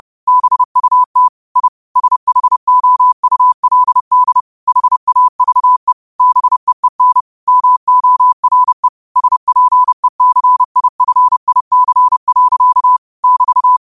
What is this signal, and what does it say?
Morse code, heard as a keyed tone, about 30 wpm, 1000 Hz. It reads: BUT I SHOULD HAVE BEEN MORE SPECIFIC. <BT>
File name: qotdfiles / 30wpm